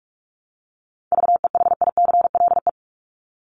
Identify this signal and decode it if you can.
Morse code, heard as a keyed tone, about 44 wpm, 700 Hz. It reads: VEHICLE
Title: Top 1751-2000 Words 45wpm